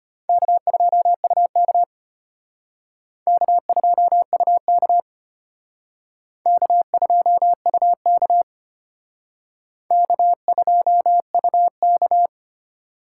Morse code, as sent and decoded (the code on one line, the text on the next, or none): -.- ..--- ..- -.- / -.- ..--- ..- -.- / -.- ..--- ..- -.- / -.- ..--- ..- -.-
K2UK K2UK K2UK K2UK